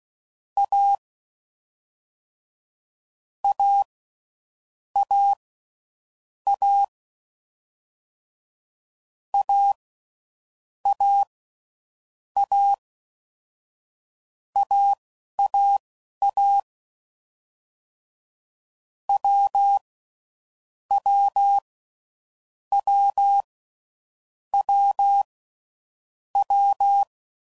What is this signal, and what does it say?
Morse code, heard as a keyed tone, about 16 wpm, spaced as for 7 wpm, 800 Hz. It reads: A AAA AAA AAA WWWWW
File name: Combinations / Difficult Combos / AWJ1